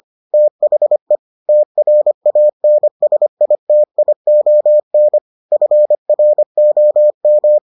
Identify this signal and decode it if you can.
Morse code, heard as a keyed tone, about 25 wpm, 600 Hz. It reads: THE TRANSITION FROM